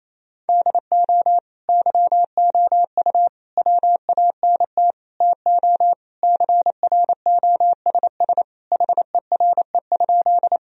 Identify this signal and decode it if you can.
Morse code, heard as a keyed tone, about 28 wpm, 700 Hz. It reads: DO YOU WANT TO CROSS HERE?
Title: Sentences from Top 1000 Words - Review Part 1 - 28wpm